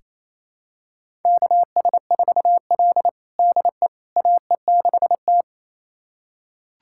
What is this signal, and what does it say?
Morse code, heard as a keyed tone, about 27 wpm, 700 Hz. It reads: KS4L DE AE6T
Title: USA Callsign DE Callsign - 4 Characters - 28wpm